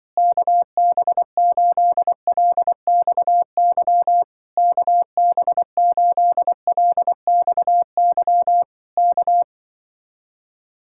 Morse code, heard as a keyed tone, about 24 wpm, 700 Hz. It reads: KB8LXY KB8LXY K
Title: Speed-Racing Common QSO Phrases Encore - 20wpm